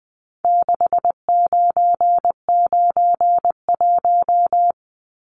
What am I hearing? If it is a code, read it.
Morse code, heard as a keyed tone, about 20 wpm, 700 Hz. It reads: 6991
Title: Numbers - 4-Digit 20wpm